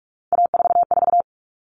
Morse code, heard as a keyed tone, about 44 wpm, 700 Hz. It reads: A44
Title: All Cut Numbers - 3-Digit 45wpm